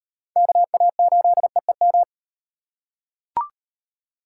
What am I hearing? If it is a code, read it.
Morse code, heard as a keyed tone, about 37 wpm, 700 Hz. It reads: KA8EEM